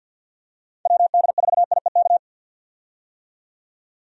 Morse code, heard as a keyed tone, about 51 wpm, 700 Hz. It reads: WD4IEK
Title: USA Call Signs - 6 Characters - 50wpm